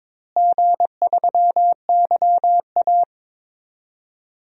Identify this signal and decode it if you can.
Morse code, heard as a keyed tone, about 22 wpm, 700 Hz. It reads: G3YA
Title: DXCC-100 Call Signs - 4 Characters - Repeated 3x - 22wpm